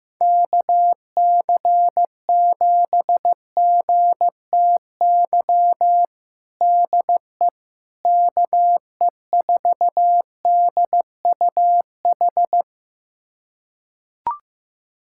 Morse code, heard as a keyed tone, about 15 wpm, 700 Hz. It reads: KC7GTY DE KE4DUH
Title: USA Callsign DE Callsign - 6 Characters - 15wpm